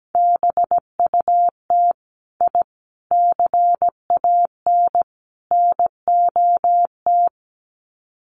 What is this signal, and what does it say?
Morse code, heard as a keyed tone, about 17 wpm, 700 Hz. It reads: BUT I CAN NOT
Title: Sets of 4 Words 17wpm